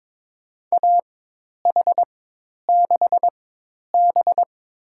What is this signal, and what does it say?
Morse code, heard as a keyed tone, about 23 wpm, spaced as for 10 wpm, 700 Hz. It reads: AH6B
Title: USA Call Signs - 4x Character Spacing - 22wpm